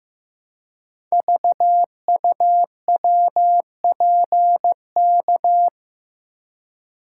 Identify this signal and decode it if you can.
Morse code, heard as a keyed tone, about 15 wpm, 700 Hz. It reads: VUWPK
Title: Code Groups - 5 Letters - 15wpm